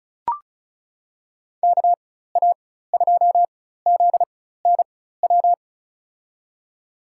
Morse code, heard as a keyed tone, about 36 wpm, spaced as for 16 wpm, 700 Hz. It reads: KA2ZNW
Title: USA Call Signs - 6 Characters - 4x Character Spacing - 35wpm